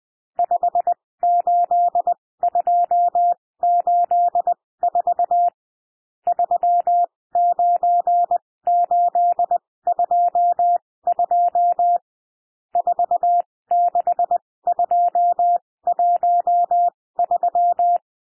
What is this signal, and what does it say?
Morse code, heard as a keyed tone, about 20 wpm, spaced as for 16 wpm, 700 Hz. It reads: 58284 39822 46213
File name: cwmp3 / N1 SOUND NUMBERS 20 wpm